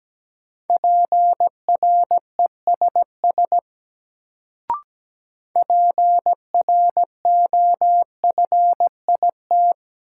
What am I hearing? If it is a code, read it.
Morse code, heard as a keyed tone, about 17 wpm, 700 Hz. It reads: PRESS PROFIT
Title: Sentences from Top 800 Words - New Word Drill - 17wpm